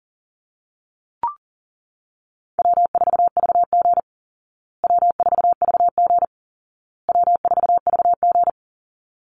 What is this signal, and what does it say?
Morse code, heard as a keyed tone, about 40 wpm, 700 Hz. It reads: W4VG W4VG W4VG
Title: Easy Call Signs Repeated 3x - 40wpm